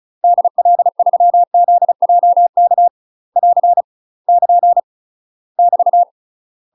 Morse code, heard as a keyed tone, about 36 wpm, 700 Hz. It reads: DL3ZJK <AR> <KN> <BT>
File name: UK MASTER QSOs TRAINING / 35WPM / UK MASTER QSOs 0016